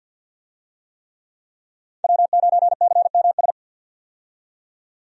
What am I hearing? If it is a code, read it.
Morse code, heard as a keyed tone, about 47 wpm, 700 Hz. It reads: W9CMS
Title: USA Call Signs - 5 Characters - 50wpm